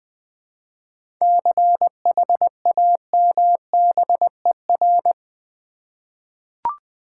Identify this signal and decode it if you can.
Morse code, heard as a keyed tone, about 20 wpm, 700 Hz. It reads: CHAMBER
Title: 7 Letter Words - First 3 Letters Given - 20wpm